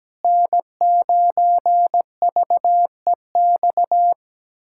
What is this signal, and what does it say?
Morse code, heard as a keyed tone, about 17 wpm, 700 Hz. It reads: N9VEX
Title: USA Callsign DE Callsign - 5 Characters - 17wpm